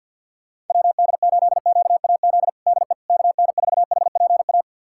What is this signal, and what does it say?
Morse code, heard as a keyed tone, about 48 wpm, 700 Hz. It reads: WD8QAZ DE KN4HPA
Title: USA Callsign DE Callsign - 6 Characters - 50wpm